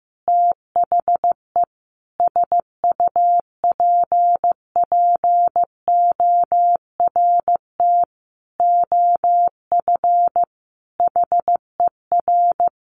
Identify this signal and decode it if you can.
Morse code, heard as a keyed tone, about 15 wpm, 700 Hz. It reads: THE SUPPORT OF HER
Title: Sentences from Top 800 Words 15wpm